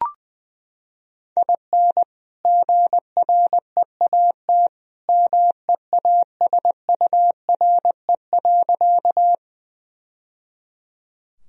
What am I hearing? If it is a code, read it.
Morse code, heard as a keyed tone, about 21 wpm, 700 Hz. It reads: IN GREAT MEASURE.
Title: Sentences from Top 1000 Words - Review Part 2 - 20wpm